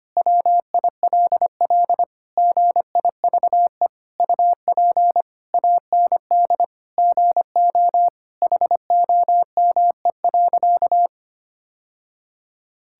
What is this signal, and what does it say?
Morse code, heard as a keyed tone, about 25 wpm, 700 Hz. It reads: WILL GIVE UP AND GO HOME.
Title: Sentences from Top 1000 Words - Review Part 6 - 25wpm